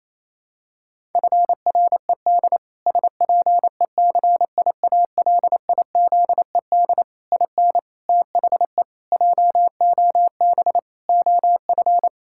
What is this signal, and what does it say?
Morse code, heard as a keyed tone, about 28 wpm, 700 Hz. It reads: FRED SPECIALIZED IN THE JOB OF
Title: Pangram Challenge v2 - 28wpm